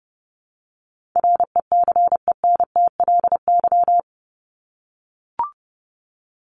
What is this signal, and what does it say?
Morse code, heard as a keyed tone, about 30 wpm, 700 Hz. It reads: RECENTLY
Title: LY Suffix - Top 100 Words with Base Words 30wpm